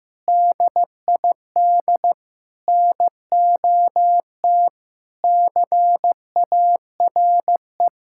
Morse code, heard as a keyed tone, about 15 wpm, 700 Hz. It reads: DID NOT CARE
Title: Sentences from Top 1000 Words - Review Part 6 - 15wpm